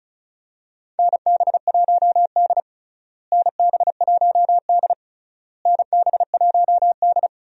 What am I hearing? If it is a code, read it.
Morse code, heard as a keyed tone, about 34 wpm, 700 Hz. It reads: NB1D NB1D NB1D
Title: USA Call Signs - 4 Characters - Repeated 3x - 35wpm